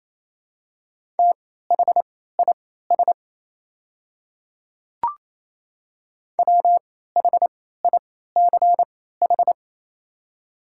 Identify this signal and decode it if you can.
Morse code, heard as a keyed tone, about 28 wpm, spaced as for 16 wpm, 690 Hz. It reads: THIS WHICH
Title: 3x Character Spacing - Top 300 Words - 28wpm